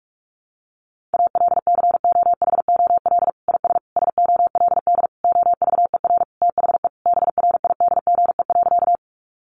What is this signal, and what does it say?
Morse code, heard as a keyed tone, about 44 wpm, 700 Hz. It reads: ALCOHOL IS SOLD OVER THE BRIDGE.
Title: Sentences from Top 1400 Words 45wpm